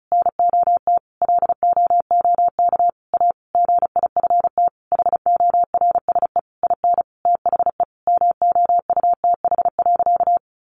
Morse code, heard as a keyed tone, about 34 wpm, 700 Hz. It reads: NOT LOOK A GIFT HORSE IN THE MOUTH.
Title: Sentences from Top 1500 Words 35wpm